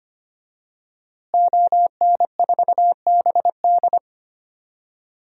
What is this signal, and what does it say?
Morse code, heard as a keyed tone, about 25 wpm, 700 Hz. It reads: ON4BD
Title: DXCC-100 Call Signs - 5 Characters - Repeated 2x - 25wpm